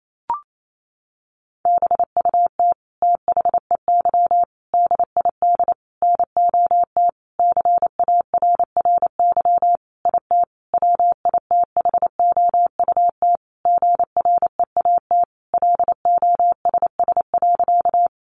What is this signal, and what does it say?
Morse code, heard as a keyed tone, about 28 wpm, 700 Hz. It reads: BUT THEY DID NOT CARRY IT WITHOUT GREAT LOSS.
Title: Sentences from Top 1400 Words 28wpm